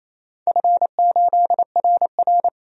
Morse code, heard as a keyed tone, about 29 wpm, 700 Hz. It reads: F8RR
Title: DXCC-100 Call Signs - 4 Characters - 28wpm